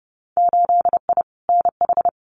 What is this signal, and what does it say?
Morse code, heard as a keyed tone, about 30 wpm, 700 Hz. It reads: 8I NH
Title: Winter Field Day Exchanges Encore 30wpm